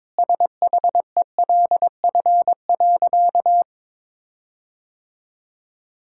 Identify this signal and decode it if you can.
Morse code, heard as a keyed tone, about 22 wpm, 700 Hz. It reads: SHELF.
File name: Sentences with 7-Word Phrases - 22wpm